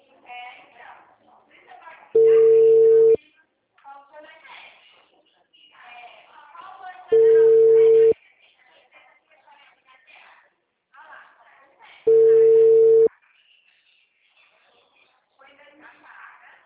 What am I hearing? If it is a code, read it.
Telephone call progress tone (China, 450 Hz), heard as a telephone network's tone, about 450 Hz, with speech. Ringback tone.